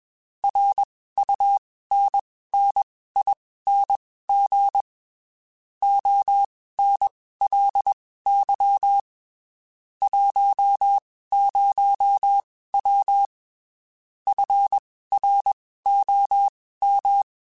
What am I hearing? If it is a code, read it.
Morse code, heard as a keyed tone, about 21 wpm, spaced as for 15 wpm, 800 Hz. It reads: RUNNING ONLY 10W FROM
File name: (spaced) / QSOs Spaced-mp3s / kc7sac @20 sp